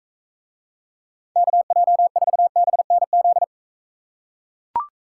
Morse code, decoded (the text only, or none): KJ4BNZ